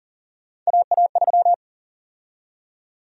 Morse code, heard as a keyed tone, about 40 wpm, 700 Hz. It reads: AA3
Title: USA Call Sign Prefixes - 40wpm